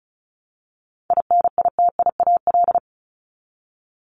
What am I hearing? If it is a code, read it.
Morse code, heard as a keyed tone, about 34 wpm, 700 Hz. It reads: INITIAL
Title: Top 1501-1750 Words 35wpm